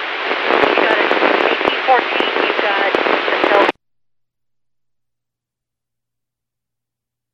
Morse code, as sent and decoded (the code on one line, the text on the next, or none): none